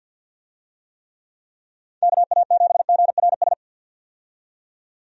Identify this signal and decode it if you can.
Morse code, heard as a keyed tone, about 53 wpm, 700 Hz. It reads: KA7GRS
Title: USA Call Signs - 6 Characters - 50wpm